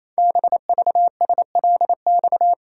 Morse code, heard as a keyed tone, about 27 wpm, 700 Hz. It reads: BVSLX
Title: Code Groups - 5 Letter:Numbers - 28wpm